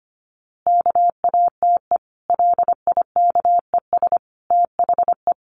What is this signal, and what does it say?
Morse code, heard as a keyed tone, about 25 wpm, 700 Hz. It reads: KATE LIKES THE
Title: Easy Phrases 25wpm